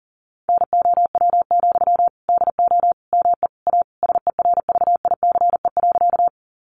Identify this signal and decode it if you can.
Morse code, heard as a keyed tone, about 37 wpm, 700 Hz. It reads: NOW, DO ME A SERVICE.
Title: Sentences from Top 700 Words 40wpm